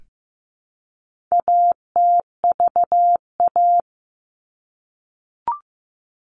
Morse code, heard as a keyed tone, about 15 wpm, 700 Hz. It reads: ATVA
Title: All Cut Numbers - 4-Digit 15wpm